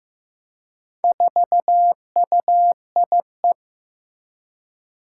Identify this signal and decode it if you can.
Morse code, heard as a keyed tone, about 15 wpm, 700 Hz. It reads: 4UIE
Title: Four Challenging Letter-Number Combinations 15wpm